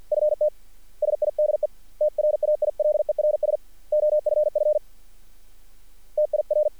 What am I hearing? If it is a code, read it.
Morse code, heard as a keyed tone, about 42 wpm, 590 Hz. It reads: FT SIDE TRAILERS OFF TIR
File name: archived / CRSnet